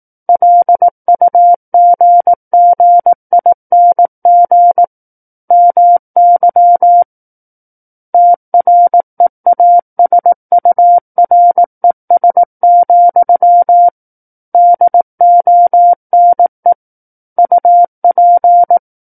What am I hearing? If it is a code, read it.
Morse code, heard as a keyed tone, about 18 wpm, 700 Hz. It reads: LUGGING MY TREASURES, DONE UP
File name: Book One / 09-Chapter 9 - 18 WPM